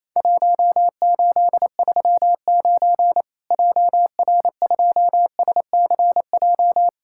Morse code, heard as a keyed tone, about 28 wpm, 700 Hz. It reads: 1839 JR2SCJ
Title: Worked All Europe DX Contest - QTC Exchange - 28wpm